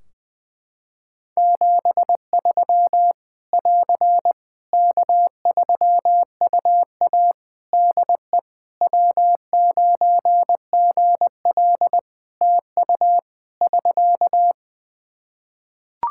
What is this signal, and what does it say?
Morse code, heard as a keyed tone, about 20 wpm, 700 Hz. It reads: 73 <AR> K3UA DE W9GL TU <SK>
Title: Practice QSO - Exchange 4 - Ending - 20wpm